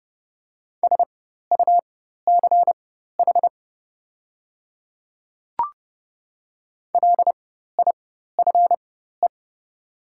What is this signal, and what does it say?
Morse code, heard as a keyed tone, about 30 wpm, spaced as for 14 wpm, 700 Hz. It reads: SUCH LIFE
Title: 4x Character Spacing - Top 300 Words - 30wpm